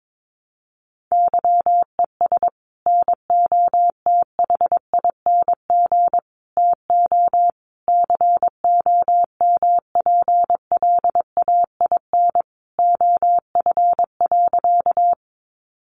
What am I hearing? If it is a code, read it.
Morse code, heard as a keyed tone, about 22 wpm, 700 Hz. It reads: YES NOTHING TO COMPLAIN OF.
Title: Sentences from Top 1300 Words 22wpm